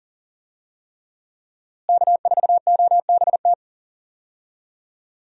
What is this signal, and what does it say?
Morse code, heard as a keyed tone, about 40 wpm, 700 Hz. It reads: K4OBT
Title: USA Call Signs - 5 Characters - Repeated 3x - 40wpm